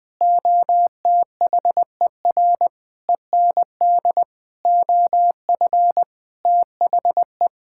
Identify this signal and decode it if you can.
Morse code, heard as a keyed tone, about 20 wpm, 700 Hz. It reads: OTHER END OF THE
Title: Sentences with 5-Word Phrases - Spoken Beginning - 20wpm